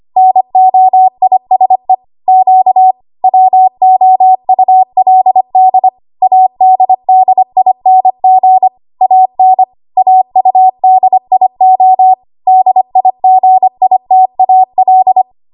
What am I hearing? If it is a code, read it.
Morse code, heard as a keyed tone, about 25 wpm, 750 Hz. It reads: NOISE Q WOULD ADDING AN AUDIO DIGITAL
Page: W1AW Web Code Practice Files